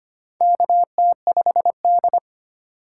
Morse code, heard as a keyed tone, about 25 wpm, 700 Hz. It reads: KT5D
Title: USA Call Signs - 4 Characters - 2x Character Spacing - 25wpm